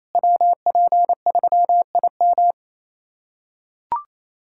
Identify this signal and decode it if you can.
Morse code, heard as a keyed tone, about 28 wpm, 700 Hz. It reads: WP3IM
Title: USA Callsign DE Callsign - 5 Characters - 28wpm